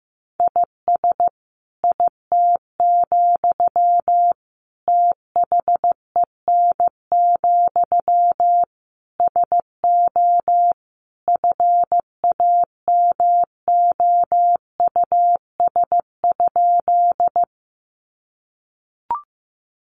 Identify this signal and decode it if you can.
Morse code, heard as a keyed tone, about 15 wpm, 700 Hz. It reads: IS IT, THEN, SO FAMOUS?
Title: Sentences from Top 1500 Words 15wpm